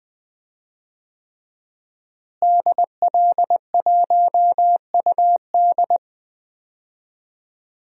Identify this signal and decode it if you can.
Morse code, heard as a keyed tone, about 20 wpm, 700 Hz. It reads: DL1UD
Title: DXCC-100 Call Signs - 5 Characters - Repeated 3x - 20wpm